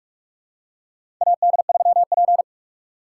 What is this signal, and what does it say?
Morse code, heard as a keyed tone, about 46 wpm, 700 Hz. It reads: AD3P
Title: USA Call Signs - 4 Characters - 3x Character Spacing - 45wpm